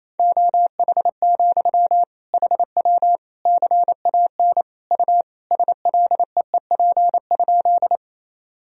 Morse code, heard as a keyed tone, about 28 wpm, 700 Hz. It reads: OH, HW CAN U SLEEP?
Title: Sentences from Top 500 Words and Abbreviations 28wpm